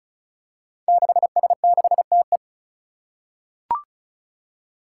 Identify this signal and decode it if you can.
Morse code, heard as a keyed tone, about 36 wpm, 700 Hz. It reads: 6S6TE
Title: Five Challenging Letter-Number Combinations 35wpm